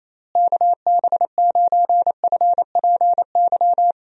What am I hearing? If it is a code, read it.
Morse code, heard as a keyed tone, about 28 wpm, 700 Hz. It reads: KB9FPY
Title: USA Call Signs - 6 Characters - Repeated 3x - 28wpm